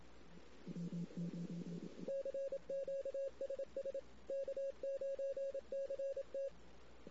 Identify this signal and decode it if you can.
Morse code, heard as a keyed tone, about 27 wpm, 550 Hz. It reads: CQ SS K9CT
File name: CW / 2008-ss-cw-k5tr-op-n5rz -16:00